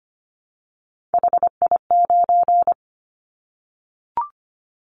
Morse code, heard as a keyed tone, about 25 wpm, 700 Hz. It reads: HI9
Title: Callsign Prefixes for Top 100 DXCC - 25wpm